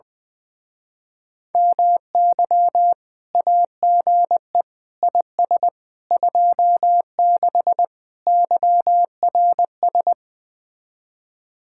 Morse code, heard as a keyed tone, about 21 wpm, 700 Hz. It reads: MY AGE IS 26 YRS
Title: Speed-Racing Common QSO Phrases Encore - 20wpm